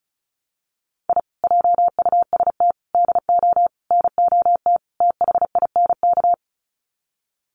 Morse code, heard as a keyed tone, about 34 wpm, 700 Hz. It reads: I JUST DO NOT THINK
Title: Sets of 5 Words Encore 35wpm